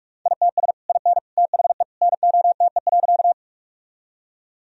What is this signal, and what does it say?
Morse code, heard as a keyed tone, about 46 wpm, 700 Hz. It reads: ITS IN THE NOTE.
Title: Sentences from Top 1000 Words - Review Part 2 - 45wpm